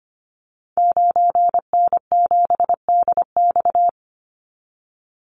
Morse code, heard as a keyed tone, about 25 wpm, 700 Hz. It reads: 9N7DX
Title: Call Signs Repeated 3x - 25wpm